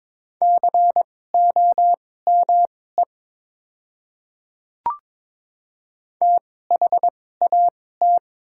Morse code, heard as a keyed tone, about 22 wpm, spaced as for 16 wpm, 700 Hz. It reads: COME THAT
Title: 2x Character Spacing - Top 300 Words - 22wpm